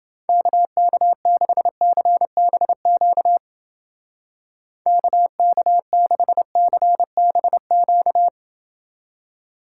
Morse code, heard as a keyed tone, about 29 wpm, 700 Hz. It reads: KK6CBQ KK6CBQ